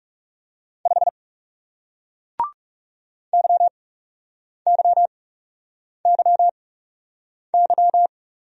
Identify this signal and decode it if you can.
Morse code, heard as a keyed tone, about 40 wpm, spaced as for 8 wpm, 700 Hz. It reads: 5 YYYY